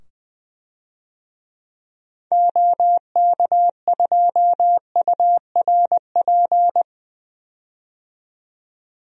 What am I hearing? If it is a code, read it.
Morse code, heard as a keyed tone, about 20 wpm, 700 Hz. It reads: OK2URP
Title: DXCC-100 Call Signs - 6 Characters - 20wpm